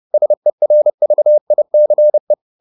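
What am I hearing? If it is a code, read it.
Morse code, heard as a keyed tone, about 30 wpm, 600 Hz. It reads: SERVICE
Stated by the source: Morse code at 30 wpm, the novel Never by Ken Follett in English, from CW med Gnister.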